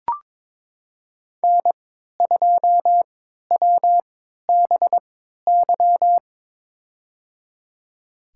Morse code, heard as a keyed tone, about 21 wpm, spaced as for 12 wpm, 700 Hz. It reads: N2WBY
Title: USA Call Signs - 5 Characters - 3x Character Spacing - 22wpm